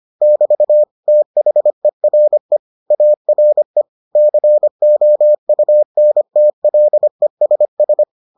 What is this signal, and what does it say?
Morse code, heard as a keyed tone, about 25 wpm, 600 Hz. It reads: <BT> THERE ARE COUNTLESS